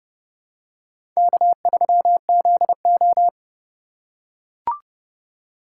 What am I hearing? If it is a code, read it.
Morse code, heard as a keyed tone, about 30 wpm, 700 Hz. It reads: K3ZO
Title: USA Call Signs - 4 Characters - 2x Character Spacing - 30wpm